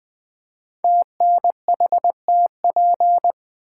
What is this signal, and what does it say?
Morse code, heard as a keyed tone, about 20 wpm, 700 Hz. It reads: TNHTP
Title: Code Groups - 5 Letters - 20wpm